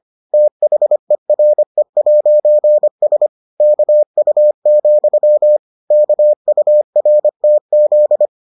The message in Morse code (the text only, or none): THERE'S KU, KURTZ